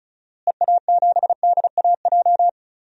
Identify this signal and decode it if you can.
Morse code, heard as a keyed tone, about 36 wpm, 700 Hz. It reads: EA7DAJ
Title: DXCC-100 Call Signs - 6 Characters - Repeated 2x - 35wpm